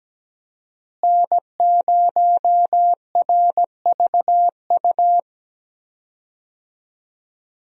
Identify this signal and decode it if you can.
Morse code, heard as a keyed tone, about 17 wpm, 700 Hz. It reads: N0RVU
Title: USA Call Signs - 5 Characters - Repeated 3x - 17wpm